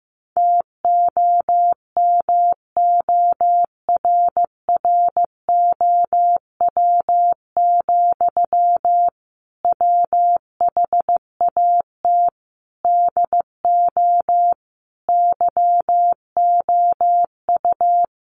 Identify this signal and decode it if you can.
Morse code, heard as a keyed tone, about 15 wpm, 700 Hz. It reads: TOMORROW, WHAT DO YOU
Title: Sentences from Top 1500 Words 15wpm